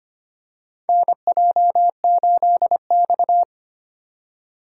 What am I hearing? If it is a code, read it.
Morse code, heard as a keyed tone, about 25 wpm, 700 Hz. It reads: NJ8X
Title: USA Call Signs - 4 Characters - 4x Character Spacing - 25wpm